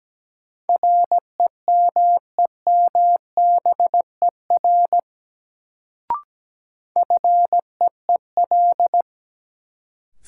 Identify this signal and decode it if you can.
Morse code, heard as a keyed tone, about 17 wpm, 700 Hz. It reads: REMEMBER FEEL
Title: Sentences from Top 400 Words - New Word Drill - 17wpm